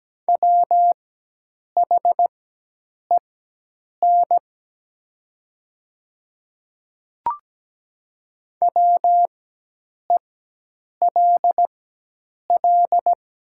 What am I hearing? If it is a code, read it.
Morse code, heard as a keyed tone, about 17 wpm, spaced as for 8 wpm, 700 Hz. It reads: WHEN WELL
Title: 4x Character Spacing - Top 200 Words - 17wpm